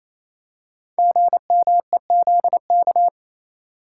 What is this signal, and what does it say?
Morse code, heard as a keyed tone, about 29 wpm, 700 Hz. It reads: GMEZK
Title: Code Groups - 5 Letter:Numbers - 28wpm